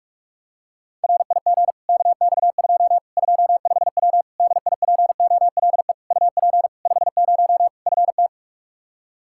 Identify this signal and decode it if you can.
Morse code, heard as a keyed tone, about 41 wpm, 700 Hz. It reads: RIG KX2 25W DIPOLE UP 50 FT
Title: K1USN SST QSO Practice 45wpm